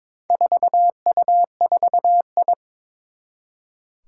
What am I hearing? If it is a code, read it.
Morse code, heard as a keyed tone, about 22 wpm, 700 Hz. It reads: 4U4I
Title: Four Challenging Letter-Number Combinations 22wpm